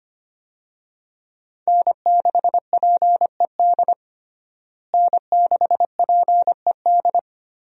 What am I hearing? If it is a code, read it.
Morse code, heard as a keyed tone, about 25 wpm, 700 Hz. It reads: N6PED N6PED
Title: USA Call Signs - 5 Characters - Repeated 2x - 25wpm